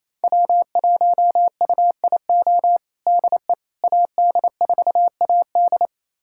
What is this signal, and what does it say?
Morse code, heard as a keyed tone, about 28 wpm, 700 Hz. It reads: W1UIO DE AD4AD
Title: USA Callsign DE Callsign - 5 Characters - 28wpm